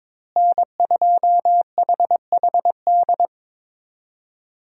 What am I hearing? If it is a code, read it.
Morse code, heard as a keyed tone, about 22 wpm, 700 Hz. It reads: N2HHD
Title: USA Call Signs - 5 Characters - Repeated 2x - 22wpm